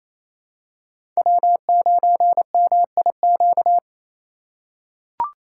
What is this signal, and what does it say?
Morse code, heard as a keyed tone, about 28 wpm, 700 Hz. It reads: W9MIQ
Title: USA Call Signs - 2x Character Spacing - 28wpm